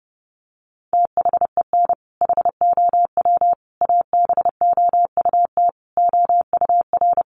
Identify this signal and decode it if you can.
Morse code, heard as a keyed tone, about 28 wpm, 700 Hz. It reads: THEN HOW ABOUT OUR
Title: Sentences from Top 900 Words 30wpm